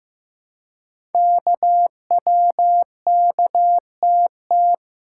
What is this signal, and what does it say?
Morse code, heard as a keyed tone, about 15 wpm, 700 Hz. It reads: KWKTT
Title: Code Groups - 5 Letters - 15wpm